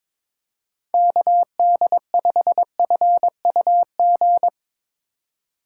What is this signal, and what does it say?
Morse code, heard as a keyed tone, about 22 wpm, 700 Hz. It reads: KD5FUG
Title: Speed Racing - USA Call Signs 22wpm